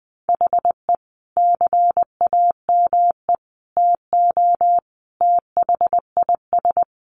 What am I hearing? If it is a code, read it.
Morse code, heard as a keyed tone, about 20 wpm, 700 Hz. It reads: HE CAME TO THIS
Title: A Sound Of Thunder 20wpm